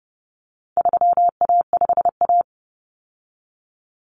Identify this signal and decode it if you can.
Morse code, heard as a keyed tone, about 30 wpm, 700 Hz. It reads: 3A5A
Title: Common Cut Numbers - 4-Digit 30wpm